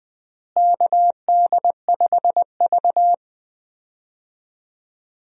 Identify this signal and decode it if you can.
Morse code, heard as a keyed tone, about 20 wpm, 700 Hz. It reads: KD5V
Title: USA Call Signs - 4 Characters - Repeated 3x - 20wpm